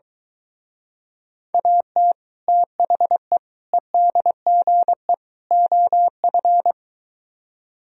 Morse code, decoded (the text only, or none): AT THE EDGE OF